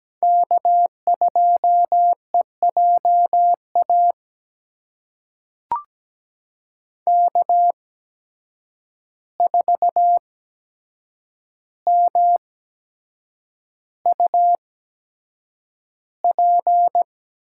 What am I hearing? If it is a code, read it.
Morse code, heard as a keyed tone, about 17 wpm, spaced as for 7 wpm, 700 Hz. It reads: K2EJA K4MUP
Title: USA Call Signs - 5 Characters - 8x Character Spacing - 17wpm